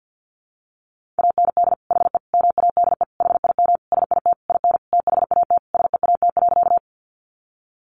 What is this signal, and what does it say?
Morse code, heard as a keyed tone, about 50 wpm, 700 Hz. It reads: AND HE MADE HIM SIT IN THAT SEAT.
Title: Sentences from Top 1500 Words 50wpm